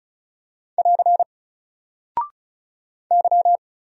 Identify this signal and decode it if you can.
Morse code, heard as a keyed tone, about 33 wpm, 700 Hz. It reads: <AR> Y
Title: Single Character (Letter:Number:Punctuation:Prosign) 35wpm